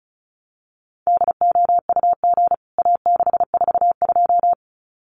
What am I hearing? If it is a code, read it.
Morse code, heard as a keyed tone, about 34 wpm, 700 Hz. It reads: DOUG A642
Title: CWT Exchange 35wpm